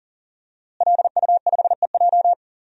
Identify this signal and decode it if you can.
Morse code, heard as a keyed tone, about 37 wpm, 700 Hz. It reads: LU5EJ
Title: DXCC-100 Call Signs - 5 Characters - Repeated 2x - 40wpm